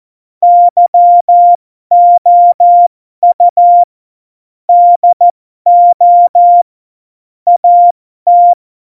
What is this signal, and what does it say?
Morse code, heard as a keyed tone, about 14 wpm, spaced as for 12 wpm, 700 Hz. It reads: YOU DO AT
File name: Morsekurs / Witze